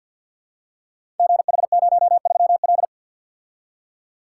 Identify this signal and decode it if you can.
Morse code, heard as a keyed tone, about 51 wpm, 690 Hz. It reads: GH03L
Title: Code Groups - 5 Letter:Numbers - 50wpm